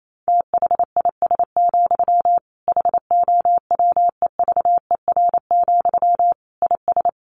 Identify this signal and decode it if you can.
Morse code, heard as a keyed tone, about 28 wpm, 700 Hz. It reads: THIS, HOWEVER, IS
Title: Sentences from Top 1200 Words 28wpm